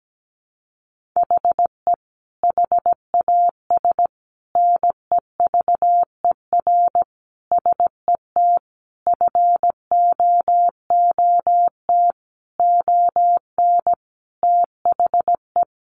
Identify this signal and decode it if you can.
Morse code, heard as a keyed tone, about 17 wpm, 700 Hz. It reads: HE HAS NEVER SET FOOT ON THE
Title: Sentences from Top 1500 Words 17wpm